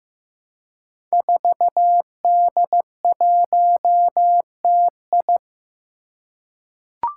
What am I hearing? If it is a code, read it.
Morse code, heard as a keyed tone, about 15 wpm, 700 Hz. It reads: 4D1TI
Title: Five Challenging Letter-Number Combinations 15wpm